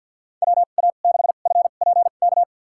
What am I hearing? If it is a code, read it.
Morse code, heard as a keyed tone, about 52 wpm, spaced as for 37 wpm, 700 Hz. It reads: WA6FPK
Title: USA Call Signs - 2x Character Spacing - 50wpm